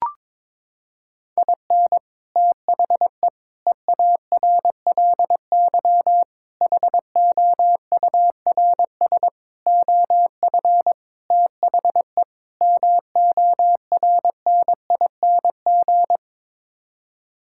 Morse code, heard as a keyed tone, about 22 wpm, 700 Hz. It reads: IN THE EARLY HOURS OF THE MORNING
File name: practice / 22wpm / Sets of 7 Words - 22wpm